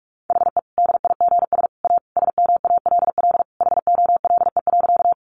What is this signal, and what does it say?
Morse code, heard as a keyed tone, about 44 wpm, 700 Hz. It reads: HE DIGS A SMALL HOLE.
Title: Easy Phrases 45wpm